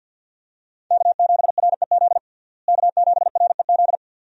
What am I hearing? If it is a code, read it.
Morse code, heard as a keyed tone, about 50 wpm, 700 Hz. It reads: K7REZ K7REZ
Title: USA Call Signs - 5 Characters - Repeated 2x - 50wpm